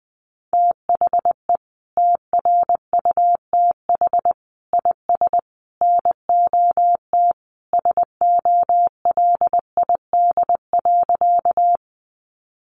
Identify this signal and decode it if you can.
Morse code, heard as a keyed tone, about 20 wpm, 700 Hz. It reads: THE TRUTH IS NOT SOLID.
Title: Sentences from Top 1500 Words 20wpm